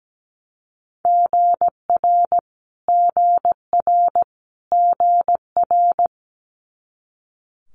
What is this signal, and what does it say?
Morse code, heard as a keyed tone, about 17 wpm, 700 Hz. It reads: GR GR GR
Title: Multi-letter Phonemes Repeated 3x 17wpm